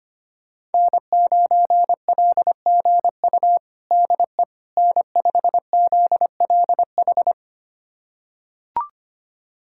Morse code, heard as a keyed tone, about 25 wpm, 700 Hz. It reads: N9LGU DE N5ZLH
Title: USA Callsign DE Callsign - 5 Characters - 25wpm